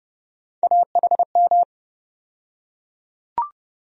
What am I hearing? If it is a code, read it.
Morse code, heard as a keyed tone, about 32 wpm, 700 Hz. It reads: AHM
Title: USA Call Sign Postfixes - 3 Letters 30wpm